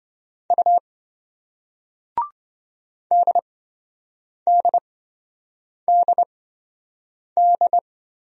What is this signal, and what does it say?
Morse code, heard as a keyed tone, about 27 wpm, spaced as for 7 wpm, 700 Hz. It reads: U DDDD